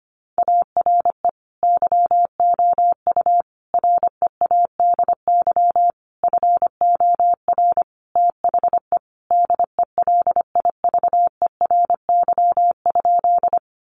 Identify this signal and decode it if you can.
Morse code, heard as a keyed tone, about 25 wpm, 700 Hz. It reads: ARE YOU READY FOR THE DELIVERY?
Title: Sentences from Top 1500 Words 25wpm